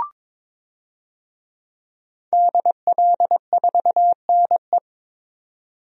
Morse code, heard as a keyed tone, about 22 wpm, 700 Hz. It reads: DL4NE
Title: DXCC-100 Call Signs - 5 Characters - 22wpm